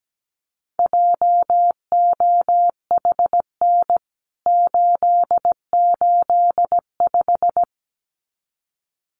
Morse code, heard as a keyed tone, about 17 wpm, 700 Hz. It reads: JOHN 885
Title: CWT Exchange 17wpm